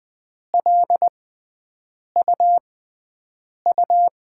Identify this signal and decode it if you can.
Morse code, heard as a keyed tone, about 20 wpm, spaced as for 7 wpm, 700 Hz. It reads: LUU
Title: USA Call Signs - 5 Characters - 6x Character Spacing - 20wpm